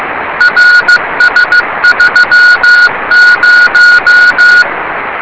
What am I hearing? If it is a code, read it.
Morse code, heard as a keyed tone, about 15 wpm, 1450 Hz. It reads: RS30
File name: rs30 cw